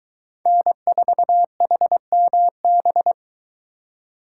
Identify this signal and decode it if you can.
Morse code, heard as a keyed tone, about 23 wpm, 700 Hz. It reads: N4HMB